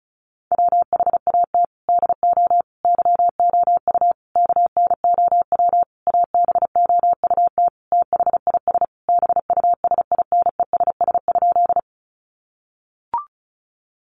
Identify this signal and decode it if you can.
Morse code, heard as a keyed tone, about 34 wpm, 700 Hz. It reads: WHAT DO YOU KNOW ABOUT THIS BUSINESS?
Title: Sentences from Top 600 Words 35wpm